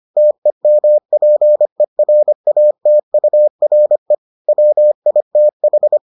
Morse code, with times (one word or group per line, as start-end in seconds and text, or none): 0.2-4.1 TEMPERATURE
4.5-6.0 WITH